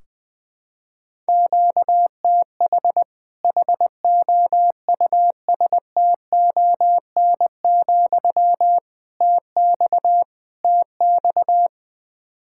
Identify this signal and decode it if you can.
Morse code, heard as a keyed tone, about 20 wpm, 700 Hz. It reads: QTH HOUSTON, TX TX
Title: Practice QSO - Exchange 1 - Beginning - 20wpm